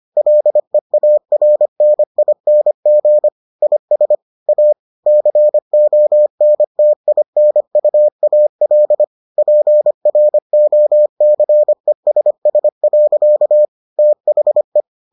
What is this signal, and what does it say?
Morse code, heard as a keyed tone, about 25 wpm, 600 Hz. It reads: LEARNING IS A CONTINUAL PROCESS. THE